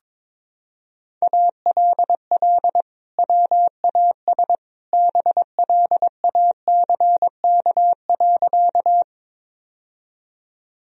Morse code, heard as a keyed tone, about 22 wpm, 700 Hz. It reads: ALL WAS BLACK.
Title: Sentences from Top 1000 Words - Review Part 2 - 22wpm